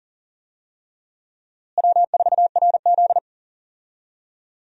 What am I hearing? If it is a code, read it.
Morse code, heard as a keyed tone, about 40 wpm, 700 Hz. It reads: W4RZ
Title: USA Call Signs - 4 Characters - Repeated 3x - 40wpm